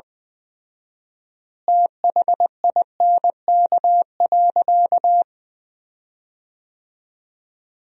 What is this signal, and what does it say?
Morse code, heard as a keyed tone, about 21 wpm, 700 Hz. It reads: THINK.
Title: 4x Word-Spacing - Sentences from Top 300 Words - 20wpm